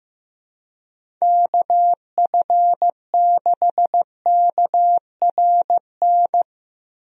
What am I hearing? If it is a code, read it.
Morse code, heard as a keyed tone, about 15 wpm, 700 Hz. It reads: KF6KRN
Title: USA Call Signs - 6 Characters - Repeated 2x - 15wpm